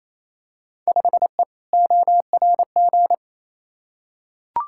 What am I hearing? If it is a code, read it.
Morse code, heard as a keyed tone, about 28 wpm, 700 Hz. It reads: 5E ORG
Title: Field Day Exchanges Encore - 28wpm